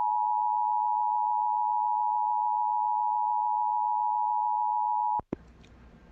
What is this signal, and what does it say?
Emergency Alert System (SAME) tones (the 853 and 960 Hz attention tone). It is heard from 0.3 to 5.1 s.